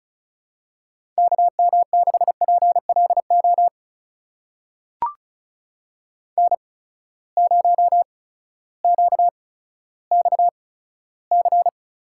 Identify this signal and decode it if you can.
Morse code, heard as a keyed tone, about 36 wpm, spaced as for 16 wpm, 700 Hz. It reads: KM6PLO N0QXC
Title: USA Call Signs - 8x Character Spacing - 35wpm